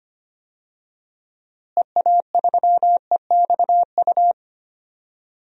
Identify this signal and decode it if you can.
Morse code, heard as a keyed tone, about 25 wpm, 700 Hz. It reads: EA3EXU
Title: DXCC-100 Call Signs - 6 Characters - Repeated 2x - 25wpm